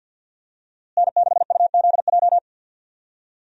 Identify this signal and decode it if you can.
Morse code, heard as a keyed tone, about 53 wpm, 700 Hz. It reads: N6UZJ
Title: USA Call Signs - 5 Characters - Repeated 3x - 50wpm